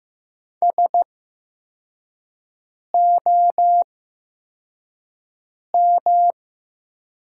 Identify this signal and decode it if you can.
Morse code, heard as a keyed tone, about 15 wpm, spaced as for 4 wpm, 700 Hz. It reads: SOM
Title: 8x Character Spacing - Top 100 Words - 15wpm